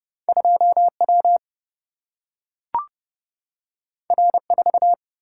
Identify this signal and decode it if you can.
Morse code, heard as a keyed tone, about 30 wpm, 700 Hz. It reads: 2W R4
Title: Single-Letter Single-Number 30wpm